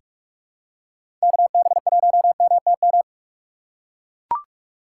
Morse code, heard as a keyed tone, about 44 wpm, 700 Hz. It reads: KB1MTM
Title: USA Call Signs - 3x Character Spacing - 45wpm